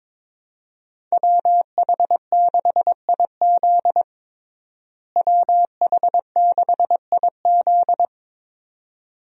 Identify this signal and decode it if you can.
Morse code, heard as a keyed tone, about 22 wpm, 700 Hz. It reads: WH6IZ WH6IZ
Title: USA Call Signs - 5 Characters - Repeated 2x - 22wpm